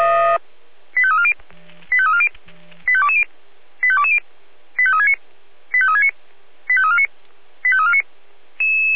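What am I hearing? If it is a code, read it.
Five-tone selective call (ZVEI), heard as quick runs of five tones, tones of about 70 ms. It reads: 85340 85340 85220 85220 85369 85369 85349 85349
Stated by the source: ZVEI five-tone selective call, received by SDR.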